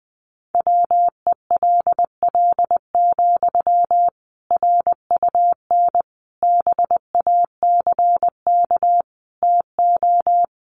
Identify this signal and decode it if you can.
Morse code, heard as a keyed tone, about 20 wpm, 700 Hz. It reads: WELL, RUN BACK TO
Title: Sentences from Top 300 Words 20wpm